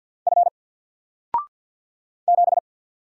Morse code, heard as a keyed tone, about 50 wpm, 700 Hz. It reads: F 7
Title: Single Letter-Number Alt 50wpm